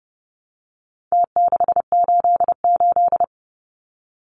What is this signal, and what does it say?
Morse code, heard as a keyed tone, about 30 wpm, 700 Hz. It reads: T688
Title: Numbers - Contest Style - 4-Digit 30wpm